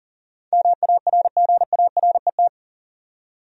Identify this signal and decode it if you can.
Morse code, heard as a keyed tone, about 40 wpm, 700 Hz. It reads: MARGARET
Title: USA Common Names - 40wpm